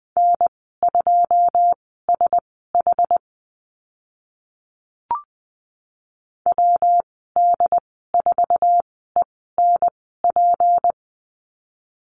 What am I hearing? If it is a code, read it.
Morse code, heard as a keyed tone, about 20 wpm, spaced as for 15 wpm, 700 Hz. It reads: N2SH WD4ENP
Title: 154 Lesson 25g - Letter p - Call Signs 20wpm